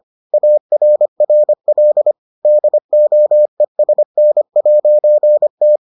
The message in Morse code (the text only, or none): ARRL DOESN'T